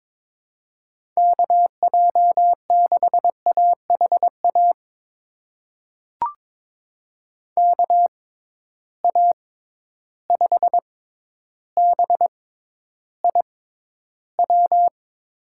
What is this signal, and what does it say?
Morse code, heard as a keyed tone, about 22 wpm, spaced as for 11 wpm, 700 Hz. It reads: KJ6AHA KA5BIW
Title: USA Call Signs - 6 Characters - 6x Character Spacing - 22wpm